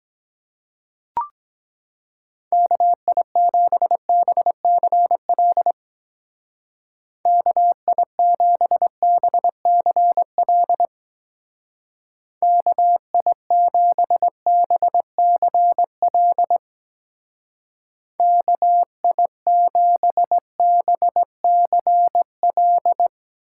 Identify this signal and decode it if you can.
Morse code, heard as a keyed tone, about 21 wpm, 700 Hz. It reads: KI7BCL KI7BCL KI7BCL KI7BCL
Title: Speed Racing - USA 2x3 Call Signs 17wpm